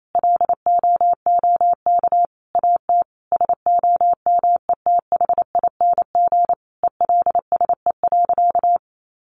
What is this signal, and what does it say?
Morse code, heard as a keyed tone, about 28 wpm, 700 Hz. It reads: LOOK AT SOMETHING ELSE.
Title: Sentences from Top 800 Words 28wpm